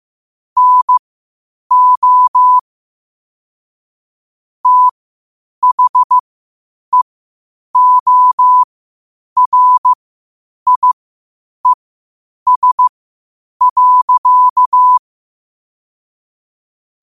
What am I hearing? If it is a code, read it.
Morse code, heard as a keyed tone, about 15 wpm, spaced as for 8 wpm, 1000 Hz. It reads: NO THEORIES.
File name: qotdfiles / 8wpm